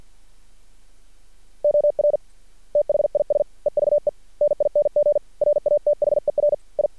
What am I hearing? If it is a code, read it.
Morse code, heard as a keyed tone, about 50 wpm, spaced as for 44 wpm, 590 Hz. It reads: OR THIS EVENING RATHER I
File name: archived / CRSnet